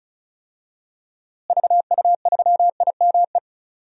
Morse code, heard as a keyed tone, about 36 wpm, 700 Hz. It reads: VU3IME
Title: DXCC-100 Call Signs - 6 Characters - Repeated 3x - 35wpm